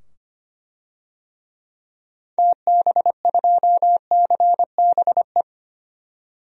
Morse code, heard as a keyed tone, about 25 wpm, 700 Hz. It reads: TB2CBE
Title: DXCC-100 Call Signs - 6 Characters - 25wpm